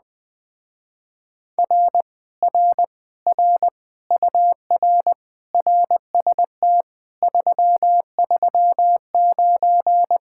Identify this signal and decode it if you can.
Morse code, heard as a keyed tone, about 20 wpm, 700 Hz. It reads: R R R UR RST 339
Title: Speed-Racing Common QSO Phrases Encore - 17wpm